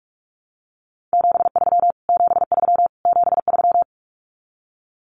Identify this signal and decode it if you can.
Morse code, heard as a keyed tone, about 44 wpm, 700 Hz. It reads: 73 73 73
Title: Abbreviations Repeated 3x - 45wpm